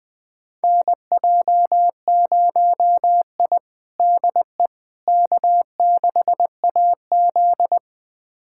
Morse code, heard as a keyed tone, about 20 wpm, 700 Hz. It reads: NJ0I DE K6AZ
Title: USA Callsign DE Callsign - 4 Characters - 20wpm